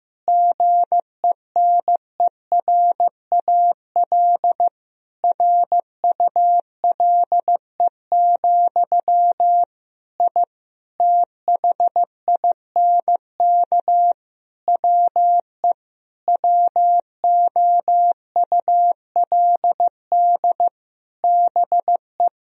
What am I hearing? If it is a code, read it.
Morse code, heard as a keyed tone, about 15 wpm, 700 Hz. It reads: GENERAL RULE, I THINK WE WOULD BE
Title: Sentences from Top 1000 Words - Review Part 3 - 15wpm